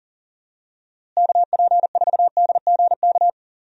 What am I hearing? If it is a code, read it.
Morse code, heard as a keyed tone, about 37 wpm, 700 Hz. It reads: KP4DGK
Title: USA Call Signs - 6 Characters - Repeated 2x - 40wpm